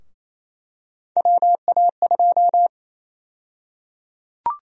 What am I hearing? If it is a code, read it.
Morse code, heard as a keyed tone, about 28 wpm, 700 Hz. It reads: WA2
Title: USA Call Sign Prefixes - Repeated 2x - 28wpm